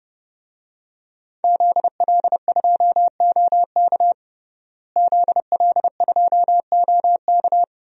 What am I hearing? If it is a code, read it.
Morse code, heard as a keyed tone, about 30 wpm, 700 Hz. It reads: ZL2OK ZL2OK
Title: DXCC-100 Call Signs - 5 Characters - Repeated 3x - 30wpm